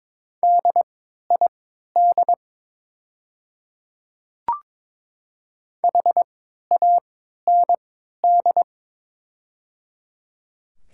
Morse code, heard as a keyed tone, about 22 wpm, spaced as for 12 wpm, 700 Hz. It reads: DID HAND
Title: 3x Character Spacing - Top 200 Words - 22wpm